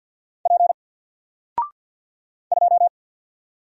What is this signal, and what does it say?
Morse code, heard as a keyed tone, about 50 wpm, 700 Hz. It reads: P 2
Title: Single Character (Letter:Number:Punctuation:Prosign) 50wpm